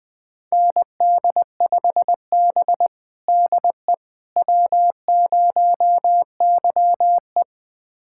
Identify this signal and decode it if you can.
Morse code, heard as a keyed tone, about 20 wpm, 700 Hz. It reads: ND5B DE W0YE
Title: USA Callsign DE Callsign - 4 Characters - 20wpm